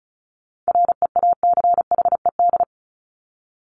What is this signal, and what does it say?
Morse code, heard as a keyed tone, about 34 wpm, 700 Hz. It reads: REACHED
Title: ED Suffix - Top 100 Words with Base Words 35wpm